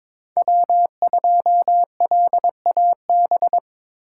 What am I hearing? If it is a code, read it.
Morse code, heard as a keyed tone, about 26 wpm, 700 Hz. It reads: W2LAB
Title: Speed Racing - USA 1x3 Call Signs 22wpm